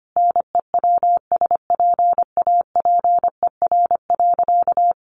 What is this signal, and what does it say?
Morse code, heard as a keyed tone, about 25 wpm, 700 Hz. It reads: NEWSPAPER.
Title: Sentences from Top 1400 Words 25wpm